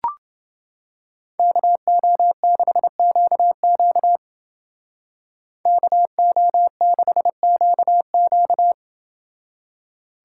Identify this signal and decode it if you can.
Morse code, heard as a keyed tone, about 28 wpm, 700 Hz. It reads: KO6QQ KO6QQ